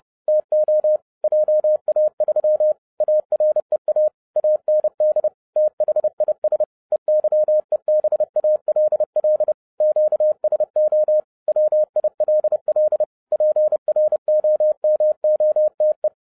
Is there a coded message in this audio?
Yes: Morse code from 0.3 to 16.1 s, heard as a keyed tone, about 30 wpm, 610 Hz. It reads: TO JA3 AREA AND THIS EYEBALL QSO WILL PROMOTE